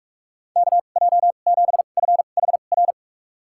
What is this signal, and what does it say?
Morse code, heard as a keyed tone, about 46 wpm, spaced as for 32 wpm, 700 Hz. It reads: KJ7FHR
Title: USA Call Signs - 6 Characters - 2x Character Spacing - 45wpm